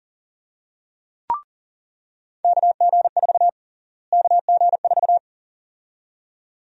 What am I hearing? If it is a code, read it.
Morse code, heard as a keyed tone, about 37 wpm, 700 Hz. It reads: KG4 KG4
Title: USA Call Sign Prefixes - Repeated 2x - 40wpm